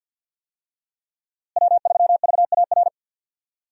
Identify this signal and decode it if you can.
Morse code, heard as a keyed tone, about 52 wpm, 700 Hz. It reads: W3VAR
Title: USA Call Signs - 5 Characters - Repeated 3x - 50wpm